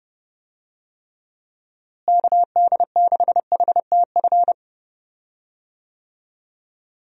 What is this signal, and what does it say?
Morse code, heard as a keyed tone, about 32 wpm, 700 Hz. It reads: KD6HTF
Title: USA Call Signs - 6 Characters - Repeated 2x - 30wpm